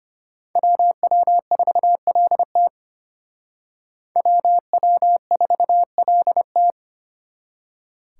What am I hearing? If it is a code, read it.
Morse code, heard as a keyed tone, about 32 wpm, 700 Hz. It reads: WW4LT WW4LT